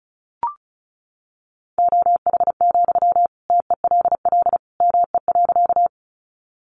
Morse code, heard as a keyed tone, about 34 wpm, 700 Hz. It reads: OH, TELL ME.
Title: Sentences from Top 500 Words 35wpm